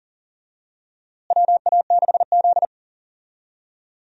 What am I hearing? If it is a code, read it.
Morse code, heard as a keyed tone, about 40 wpm, 700 Hz. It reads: WA6Z
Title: USA Call Signs - 4 Characters - Repeated 3x - 40wpm